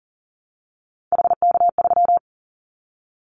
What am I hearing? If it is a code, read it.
Morse code, heard as a keyed tone, about 40 wpm, 700 Hz. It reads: HK3
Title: Callsign Prefixes for Top 100 DXCC - 40wpm